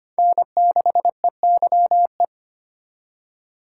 Morse code, heard as a keyed tone, about 25 wpm, 700 Hz. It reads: N6EYE
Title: USA Call Signs - 5 Characters - 4x Character Spacing - 25wpm